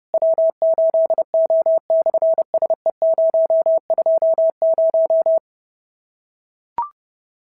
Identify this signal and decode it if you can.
Morse code, heard as a keyed tone, about 30 wpm, 650 Hz. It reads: W8O/SE020
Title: SOTA QSO Practice 30wpm